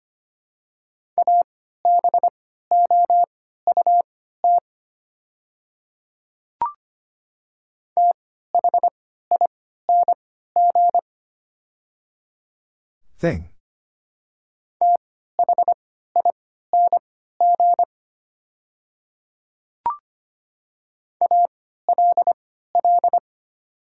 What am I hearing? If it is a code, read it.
Morse code, heard as a keyed tone, about 25 wpm, spaced as for 14 wpm, 700 Hz. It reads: ABOUT THING THING ALL
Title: 3x Character Spacing - Top 100 Words - 25wpm